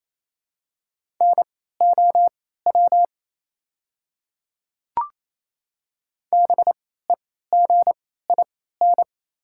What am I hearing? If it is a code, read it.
Morse code, heard as a keyed tone, about 27 wpm, spaced as for 16 wpm, 690 Hz. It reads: NOW BEGIN
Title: 3x Character Spacing - Top 300 Words - 28wpm